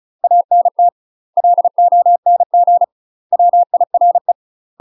Morse code, heard as a keyed tone, about 34 wpm, 700 Hz. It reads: ANT LONG WIRE
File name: UK MASTER QSOs TRAINING / 35WPM / UK MASTER QSOs 0042